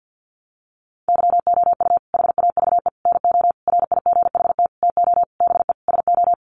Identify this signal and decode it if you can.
Morse code, heard as a keyed tone, about 50 wpm, 700 Hz. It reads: YOU HAVE NO RIGHT TO BE SO
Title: Sentences from Top 1100 Words 50wpm